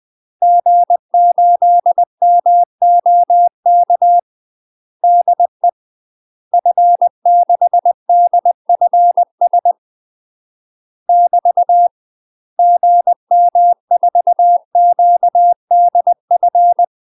Morse code, heard as a keyed tone, about 20 wpm, 700 Hz. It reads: G8MOK DE F6DFS <BT> GM4QDF
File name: UK MASTER QSOs TRAINING / 20WPM / UK MASTER QSOs 0058